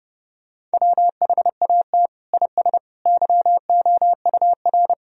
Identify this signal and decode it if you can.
Morse code, heard as a keyed tone, about 30 wpm, 700 Hz. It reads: WHAT IS YOUR
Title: Sentences from Top 1000 Words - Review Part 4 - 30wpm